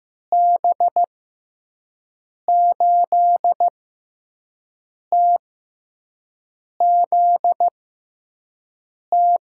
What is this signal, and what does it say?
Morse code, heard as a keyed tone, about 15 wpm, spaced as for 5 wpm, 700 Hz. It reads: B8TZT
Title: USA Call Signs - 6 Characters - 6x Character Spacing - 15wpm